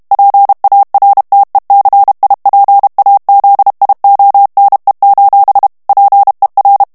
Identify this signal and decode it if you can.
Morse code, heard as a keyed tone, about 32 wpm, 800 Hz. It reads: PARTECIPAZIONE: PER
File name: texte clair 2 vitesse 30 mots minute